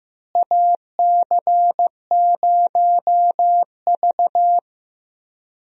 Morse code, heard as a keyed tone, about 15 wpm, 700 Hz. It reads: AC0V
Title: USA Call Signs - 4 Characters - Repeated 3x - 15wpm